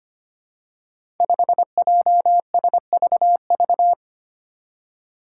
Morse code, heard as a keyed tone, about 25 wpm, 700 Hz. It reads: 5JSVV
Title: Five Challenging Letter-Number Combinations 25wpm